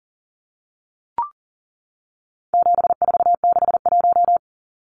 Morse code, heard as a keyed tone, about 40 wpm, 700 Hz. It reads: 7461
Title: Numbers - 4-Digit 40wpm